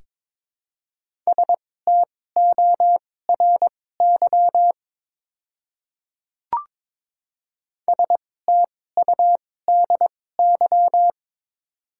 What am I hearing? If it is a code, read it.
Morse code, heard as a keyed tone, about 22 wpm, spaced as for 16 wpm, 700 Hz. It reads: STORY STUDY
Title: 2x Character Spacing - Top 300 Words - 22wpm